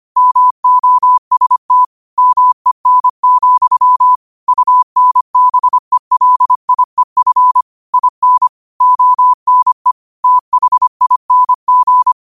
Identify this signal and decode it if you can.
Morse code, heard as a keyed tone, about 25 wpm, 1000 Hz. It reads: MOST MEN, UNBELIEF IN ONE THING
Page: New quotes every day in morse code at 25 Words per minute.